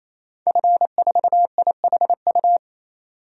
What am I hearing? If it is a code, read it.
Morse code, heard as a keyed tone, about 28 wpm, 700 Hz. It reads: F4IHU
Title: DXCC-100 Call Signs - 5 Characters - 28wpm